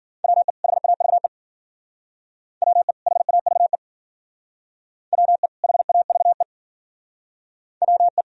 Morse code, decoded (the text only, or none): WE HAVE WE HAVE WE HAVE WE